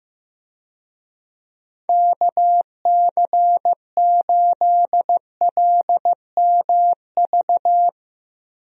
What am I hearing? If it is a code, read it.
Morse code, heard as a keyed tone, about 16 wpm, 700 Hz. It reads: KC8LMV
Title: USA Call Signs - 6 Characters - Repeated 2x - 15wpm